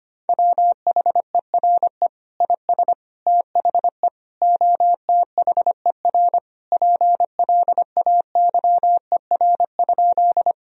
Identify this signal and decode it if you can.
Morse code, heard as a keyed tone, about 25 wpm, 700 Hz. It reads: WHERE IS THE OTHER PLAYER?
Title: Sentences from Top 1000 Words - Review Part 4 - 25wpm